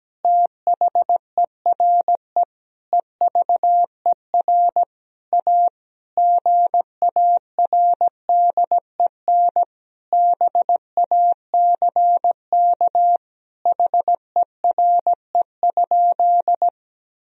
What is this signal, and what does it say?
Morse code, heard as a keyed tone, about 17 wpm, 700 Hz. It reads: THERE EVER A GARDEN BACK HERE?
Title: Sentences from Top 1000 Words - Review Part 4 - 17wpm